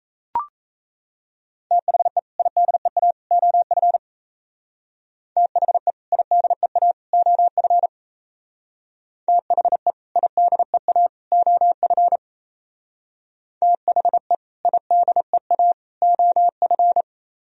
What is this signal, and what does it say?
Morse code, heard as a keyed tone, about 34 wpm, 700 Hz. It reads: THE IDEA OF THE IDEA OF THE IDEA OF THE IDEA OF
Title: Speed-Racing - Sets of 3 Words 28wpm